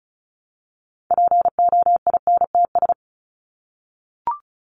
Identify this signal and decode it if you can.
Morse code, heard as a keyed tone, about 34 wpm, 700 Hz. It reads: POINTS
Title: ES:S Suffix - Top 100 Words with Base Words 35wpm